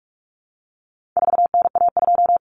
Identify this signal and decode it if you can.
Morse code, heard as a keyed tone, about 44 wpm, 700 Hz. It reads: 4NA2
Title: Common Cut Numbers - 4-Digit 45wpm